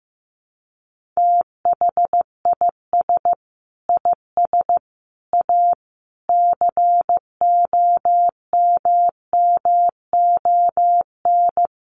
Morse code, heard as a keyed tone, about 14 wpm, 700 Hz. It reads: THIS IS A COMMON
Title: Sentences from Top 1500 Words 15wpm